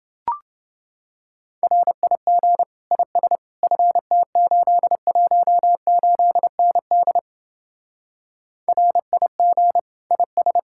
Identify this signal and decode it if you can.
Morse code, heard as a keyed tone, about 29 wpm, 700 Hz. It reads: RIG IS FT818ND RIG IS